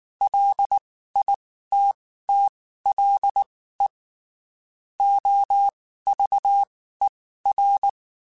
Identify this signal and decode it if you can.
Morse code, heard as a keyed tone, about 19 wpm, spaced as for 14 wpm, 800 Hz. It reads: LITTLE OVER
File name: (spaced) / QSOs Spaced-mp3s / kc7haa @18 sp